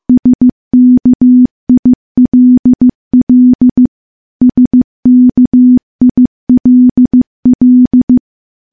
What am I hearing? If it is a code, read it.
Morse code, heard as a keyed tone, about 15 wpm, 250 Hz. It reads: SKILL SKILL
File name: morse / morse-training-15WPM